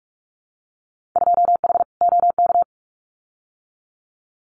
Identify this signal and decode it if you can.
Morse code, heard as a keyed tone, about 46 wpm, 700 Hz. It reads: 2H OK
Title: Winter Field Day Exchanges Encore 45wpm